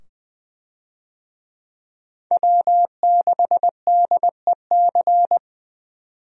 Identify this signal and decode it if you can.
Morse code, heard as a keyed tone, about 20 wpm, 700 Hz. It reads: W6DEC
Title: USA Call Signs - 5 Characters - 20wpm